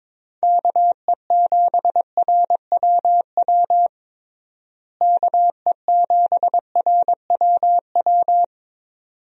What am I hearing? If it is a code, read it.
Morse code, heard as a keyed tone, about 23 wpm, 700 Hz. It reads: KE7RWW KE7RWW
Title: USA Call Signs - 6 Characters - Repeated 2x - 22wpm